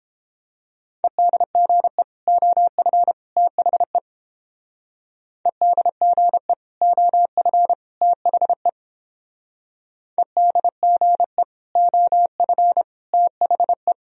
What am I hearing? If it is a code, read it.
Morse code, heard as a keyed tone, about 30 wpm, 700 Hz. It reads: EDGE OF THE EDGE OF THE EDGE OF THE